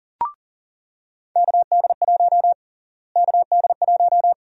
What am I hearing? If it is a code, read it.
Morse code, heard as a keyed tone, about 40 wpm, 700 Hz. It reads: KD1 KD1
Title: USA Call Sign Prefixes - Repeated 2x - 40wpm